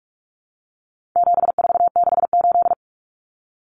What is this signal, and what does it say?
Morse code, heard as a keyed tone, about 44 wpm, 700 Hz. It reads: 7468
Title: Numbers - 4-Digit 45wpm